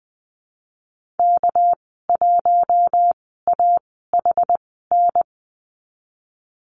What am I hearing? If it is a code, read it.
Morse code, heard as a keyed tone, about 20 wpm, spaced as for 15 wpm, 700 Hz. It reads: K1AHN
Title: 245 Lesson 39g - Letter z - Call Signs 20wpm